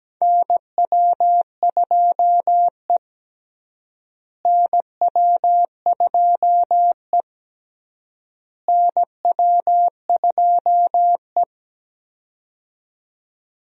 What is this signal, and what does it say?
Morse code, heard as a keyed tone, about 17 wpm, 700 Hz. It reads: NW2E NW2E NW2E
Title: USA Call Signs - 4 Characters - Repeated 3x - 17wpm